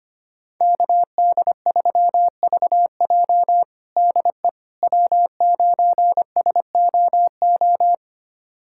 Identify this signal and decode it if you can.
Morse code, heard as a keyed tone, about 25 wpm, 700 Hz. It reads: KD3VJ DE W9SOO
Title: USA Callsign DE Callsign - 5 Characters - 25wpm